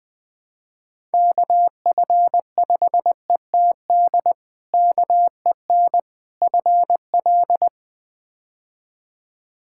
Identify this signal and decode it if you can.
Morse code, heard as a keyed tone, about 20 wpm, 700 Hz. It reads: KF5ETD KEN FL
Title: K1USN SST QSO Practice - 20wpm